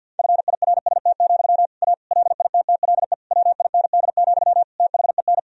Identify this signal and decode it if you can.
Morse code, heard as a keyed tone, about 52 wpm, 700 Hz. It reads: FIRST, A LITTLE WIND, THEN
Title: Sentences from Top 1000 Words - Review Part 2 - 50wpm